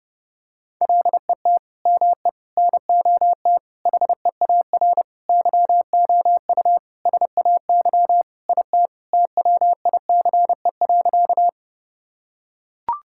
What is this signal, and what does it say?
Morse code, heard as a keyed tone, about 28 wpm, 700 Hz. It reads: LET ME NOT HEAR YOU SAY IT TWICE.
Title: Sentences from Top 1600 Words 30wpm